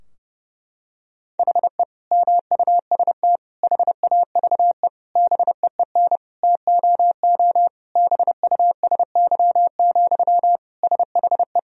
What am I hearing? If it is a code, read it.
Morse code, heard as a keyed tone, about 30 wpm, 700 Hz. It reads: HE MUST HAVE BEEN TOO BUSY, SHE
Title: Sentences from Top 1000 Words - Review Part 3 - 30wpm